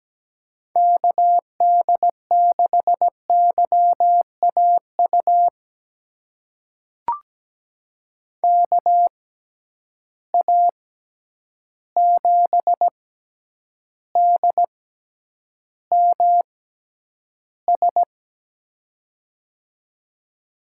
Morse code, heard as a keyed tone, about 17 wpm, spaced as for 9 wpm, 700 Hz. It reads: KD6YAU KA7DMS
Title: USA Call Signs - 6 Characters - 6x Character Spacing - 17wpm